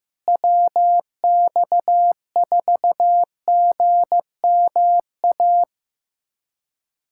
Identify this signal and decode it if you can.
Morse code, heard as a keyed tone, about 15 wpm, 700 Hz. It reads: WX4GMA
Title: USA Callsign DE Callsign - 6 Characters - 15wpm